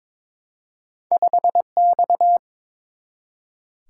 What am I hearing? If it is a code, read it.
Morse code, heard as a keyed tone, about 22 wpm, 700 Hz. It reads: 5X
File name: practice / 22wpm / Two Letter-Number Combinations - 22wpm